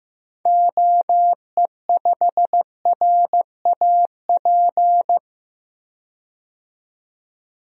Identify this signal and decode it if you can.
Morse code, heard as a keyed tone, about 16 wpm, 700 Hz. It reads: OE5RAP
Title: DXCC-100 Call Signs - 6 Characters - 15wpm